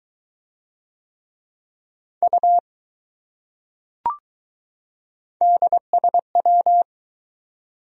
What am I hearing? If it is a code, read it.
Morse code, heard as a keyed tone, about 22 wpm, 700 Hz. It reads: U DSW